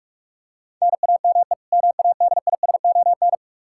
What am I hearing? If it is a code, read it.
Morse code, heard as a keyed tone, about 46 wpm, 700 Hz. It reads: NAME MADISON
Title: Practice QSO - Exchange 1 - Beginning - 45wpm